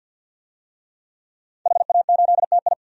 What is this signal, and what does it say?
Morse code, heard as a keyed tone, about 52 wpm, 700 Hz. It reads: HA8TI
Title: DXCC-100 Call Signs - 5 Characters - Repeated 3x - 50wpm